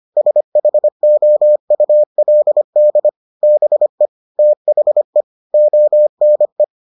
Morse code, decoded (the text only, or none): SHOULD BE THE ONE